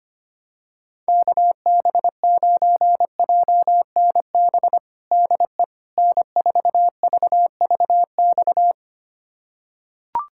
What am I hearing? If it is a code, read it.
Morse code, heard as a keyed tone, about 25 wpm, 700 Hz. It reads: KB9JNB DE N4VVX
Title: USA Callsign DE Callsign 25wpm